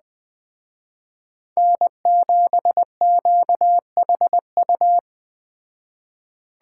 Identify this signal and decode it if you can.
Morse code, heard as a keyed tone, about 20 wpm, 700 Hz. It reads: N7QHU
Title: Speed Racing - USA 1x3 Call Signs 17wpm